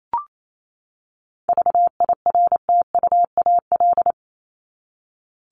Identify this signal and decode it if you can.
Morse code, heard as a keyed tone, about 28 wpm, 700 Hz. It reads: VIRTUAL
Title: LY Suffix - Top 100 Words with Base Words 28wpm